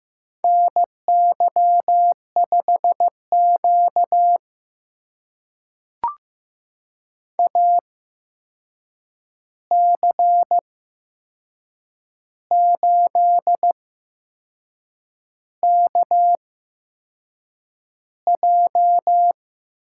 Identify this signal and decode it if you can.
Morse code, heard as a keyed tone, about 15 wpm, spaced as for 6 wpm, 700 Hz. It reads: NY5Q AC8KJ
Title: USA Call Signs - 8x Character Spacing - 15wpm